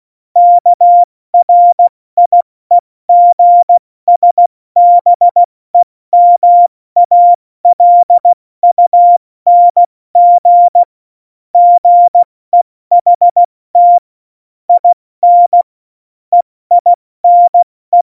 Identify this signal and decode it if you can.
Morse code, heard as a keyed tone, about 16 wpm, 700 Hz. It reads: KRIEGSBEMALUNG GEHT IN EINE
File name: Morsekurs / Witze / Witz 04. September 2018, 70 F80